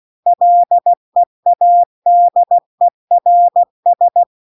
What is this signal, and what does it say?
Morse code, heard as a keyed tone, about 16 wpm, 700 Hz. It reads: LEADERS